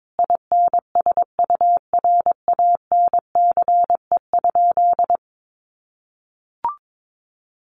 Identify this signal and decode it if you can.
Morse code, heard as a keyed tone, about 22 wpm, 700 Hz. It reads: INSURANCE?
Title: Sentences from Top 1100 Words 22wpm